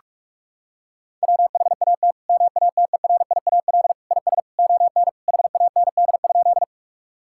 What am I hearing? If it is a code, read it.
Morse code, heard as a keyed tone, about 45 wpm, 700 Hz. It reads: WHAT MATERIAL IS ON HAND?
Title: Sentences from Top 1000 Words - Review Part 5 - 45wpm